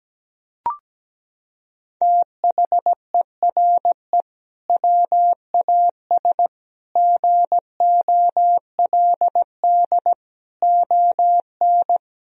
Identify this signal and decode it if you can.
Morse code, heard as a keyed tone, about 17 wpm, 700 Hz. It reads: THERE WAS GOLD ON
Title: Sentences from Top 1000 Words - Review Part 4 - 17wpm